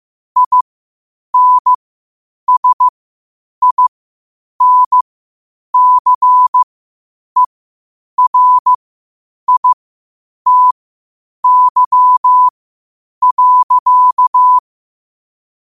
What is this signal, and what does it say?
Morse code, heard as a keyed tone, about 15 wpm, spaced as for 8 wpm, 1000 Hz. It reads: INSINCERITY.
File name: qotdfiles / 8wpm